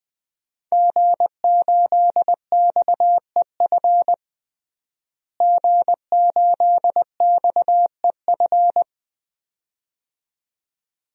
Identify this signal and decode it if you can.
Morse code, heard as a keyed tone, about 20 wpm, 700 Hz. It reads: G8XEF G8XEF
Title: DXCC-100 Call Signs - 5 Characters - Repeated 2x - 20wpm